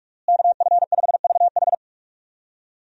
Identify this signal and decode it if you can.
Morse code, heard as a keyed tone, about 46 wpm, 700 Hz. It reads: KF5VH
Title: USA Call Signs - 5 Characters - Repeated 3x - 45wpm